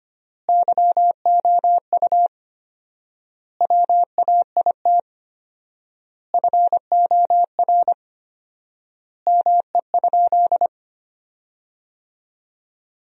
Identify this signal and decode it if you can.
Morse code, heard as a keyed tone, about 25 wpm, 700 Hz. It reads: YOU WAIT FOR ME?
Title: 4x Word-Spacing - Sentences from Top 500 Words - 25wpm